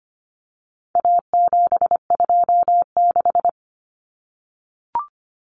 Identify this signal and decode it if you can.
Morse code, heard as a keyed tone, about 25 wpm, 700 Hz. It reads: A726
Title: Common Cut Numbers - 4-Digit 25wpm